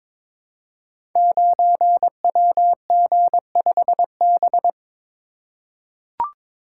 Morse code, heard as a keyed tone, about 22 wpm, 700 Hz. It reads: 9WG5B
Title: Code Groups - 5 Letter:Numbers - 22wpm